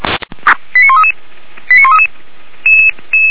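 Five-tone selective call (ZVEI), heard as quick runs of five tones, tones of about 70 ms. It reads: 89133 89133
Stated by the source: ZVEI five-tone selective call, code 89132, received by SDR; recording hosted by feuerwehr-orlamuende.org (fire brigade Orlamünde).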